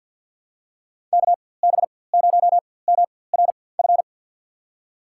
Morse code, heard as a keyed tone, about 51 wpm, spaced as for 23 wpm, 700 Hz. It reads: KB0MRF
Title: USA Call Signs - 6 Characters - 4x Character Spacing - 50wpm